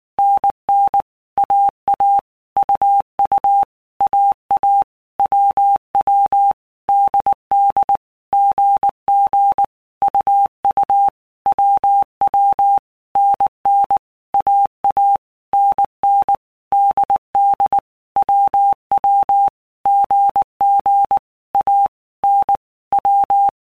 Morse code, heard as a keyed tone, about 19 wpm, 800 Hz. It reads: NN AA UU AA WW DD GG UU WW NN AA NN DD WW GG A N W
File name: Basic Letter Groups By Blocks / Groups@18 / AWUNDG@18